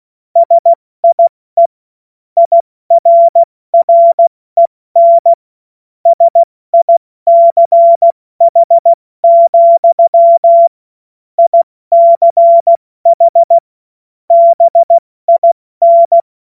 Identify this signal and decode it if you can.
Morse code, heard as a keyed tone, about 16 wpm, 680 Hz. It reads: SIE IRREN SICH, ICH BIN